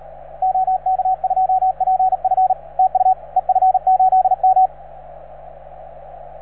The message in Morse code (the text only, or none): OK2PF TU EF8M